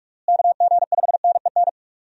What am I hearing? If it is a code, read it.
Morse code, heard as a keyed tone, about 45 wpm, 700 Hz. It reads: KG5NEN
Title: USA Call Signs - 6 Characters - Repeated 2x - 45wpm